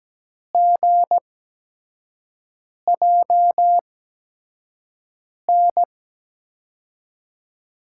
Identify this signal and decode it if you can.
Morse code, heard as a keyed tone, about 17 wpm, spaced as for 5 wpm, 700 Hz. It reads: GJN
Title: USA Call Signs - 6 Characters - 8x Character Spacing - 17wpm